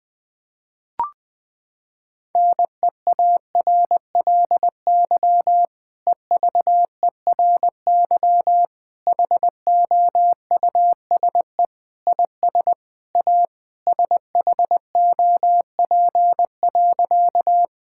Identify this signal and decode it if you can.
Morse code, heard as a keyed tone, about 20 wpm, 700 Hz. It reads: NEARLY EVERY HOUSE IS A SHOP.
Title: Sentences from Top 1600 Words 20wpm